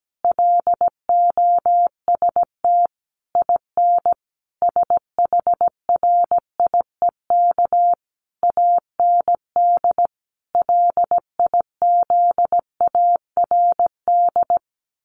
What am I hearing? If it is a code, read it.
Morse code, heard as a keyed tone, about 17 wpm, 700 Hz. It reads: LOST IN SHRIEK AND LIZARD
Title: A Sound Of Thunder 17wpm